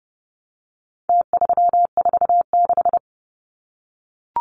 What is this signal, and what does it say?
Morse code, heard as a keyed tone, about 30 wpm, 700 Hz. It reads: T346
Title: Numbers - Contest Style - 4-Digit 30wpm